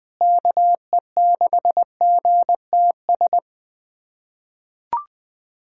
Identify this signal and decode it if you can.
Morse code, heard as a keyed tone, about 20 wpm, 700 Hz. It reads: KE6GTS
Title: USA Callsign DE Callsign - 6 Characters - 20wpm